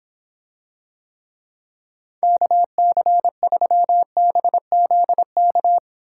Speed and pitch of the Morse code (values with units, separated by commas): 26 wpm, 700 Hz